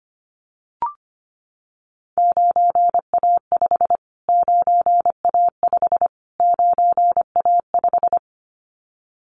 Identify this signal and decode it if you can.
Morse code, heard as a keyed tone, about 25 wpm, 700 Hz. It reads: 9A5 9A5 9A5
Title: Callsign Prefixes for Top 25 DXCC - Repeated 3x - 25wpm